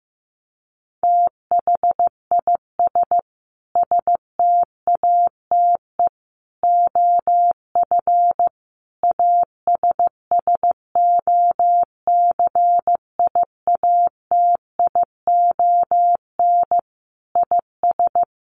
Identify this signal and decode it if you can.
Morse code, heard as a keyed tone, about 15 wpm, 700 Hz. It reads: THIS STATE OF ASSOCIATION IS
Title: Sentences from Top 1100 Words 15wpm